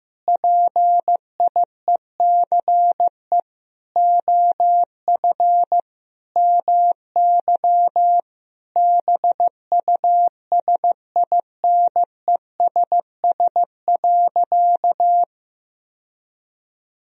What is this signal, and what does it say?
Morse code, heard as a keyed tone, about 15 wpm, 700 Hz. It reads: PIECE OF MY BUSINESS.
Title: Sentences from Top 1000 Words - Review Part 2 - 15wpm